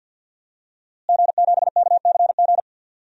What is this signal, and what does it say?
Morse code, heard as a keyed tone, about 47 wpm, 700 Hz. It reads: G7KCG
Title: DXCC-100 Call Signs - 5 Characters - Repeated 3x - 50wpm